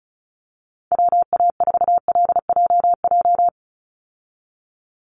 Morse code, heard as a keyed tone, about 34 wpm, 700 Hz. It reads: WA4LJJ
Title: Call Signs - 35wpm